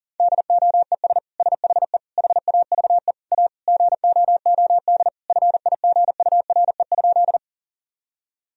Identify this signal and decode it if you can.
Morse code, heard as a keyed tone, about 40 wpm, 700 Hz. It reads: DOES SHE HAVE A GOOD FIGURE?
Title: Sentences from Top 1000 Words - Review Part 2 - 40wpm